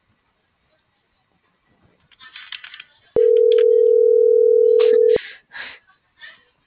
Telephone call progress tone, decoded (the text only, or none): ringback tone